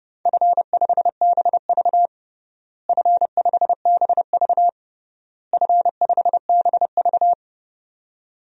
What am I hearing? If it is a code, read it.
Morse code, heard as a keyed tone, about 30 wpm, 700 Hz. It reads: F5BV F5BV F5BV
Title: DXCC-100 Call Signs - 4 Characters - Repeated 3x - 30wpm